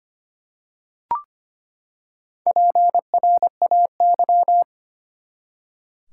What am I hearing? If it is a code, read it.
Morse code, heard as a keyed tone, about 25 wpm, 700 Hz. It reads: PRAY
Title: Four Letter Words - 25wpm